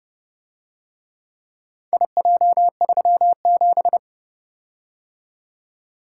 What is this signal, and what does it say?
Morse code, heard as a keyed tone, about 29 wpm, 700 Hz. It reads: IJ37